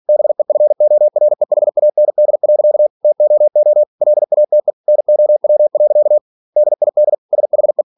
Morse code, heard as a keyed tone, about 46 wpm, 600 Hz. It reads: BEFOREHAND. TOO LATE NOW. DID SHE
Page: Never 47wpm